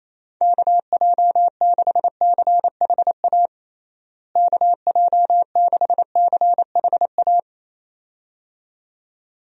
Morse code, heard as a keyed tone, about 28 wpm, 700 Hz. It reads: KJ6CHA KJ6CHA
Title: USA Call Signs - 6 Characters - Repeated 2x - 28wpm